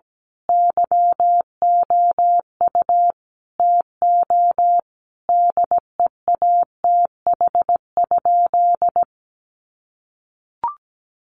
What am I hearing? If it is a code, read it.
Morse code, heard as a keyed tone, about 17 wpm, 700 Hz. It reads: YOU TO DEATH?
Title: Sentences from Top 1100 Words 17wpm